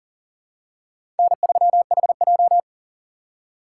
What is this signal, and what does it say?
Morse code, heard as a keyed tone, about 40 wpm, 700 Hz. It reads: N3HJ
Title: USA Call Signs - 4 Characters - 2x Character Spacing - 40wpm